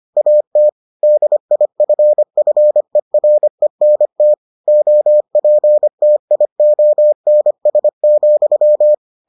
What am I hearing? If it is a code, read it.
Morse code, heard as a keyed tone, about 25 wpm, 600 Hz. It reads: AT DIFFERENT OPTIONS,